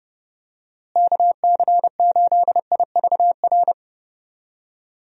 Morse code, heard as a keyed tone, about 30 wpm, 700 Hz. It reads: KC8IVR
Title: USA Call Signs - 6 Characters - Repeated 2x - 30wpm